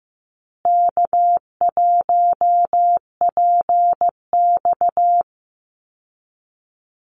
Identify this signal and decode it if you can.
Morse code, heard as a keyed tone, about 15 wpm, 700 Hz. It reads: K1PX
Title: Easy Call Signs Repeated 3x - 15wpm